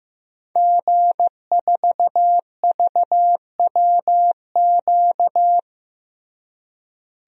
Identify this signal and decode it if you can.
Morse code, heard as a keyed tone, about 16 wpm, 700 Hz. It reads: G4VWQ
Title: DXCC-100 Call Signs - 5 Characters - Repeated 2x - 15wpm